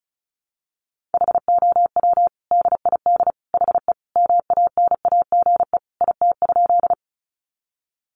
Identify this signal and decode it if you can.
Morse code, heard as a keyed tone, about 34 wpm, 700 Hz. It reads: HOW DID HE MANAGE IT?
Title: Sentences from Top 1400 Words 35wpm